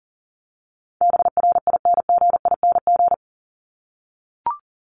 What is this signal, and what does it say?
Morse code, heard as a keyed tone, about 40 wpm, 700 Hz. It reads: BRINGING
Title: ING Suffix - Top 100 Words with Base Words 40wpm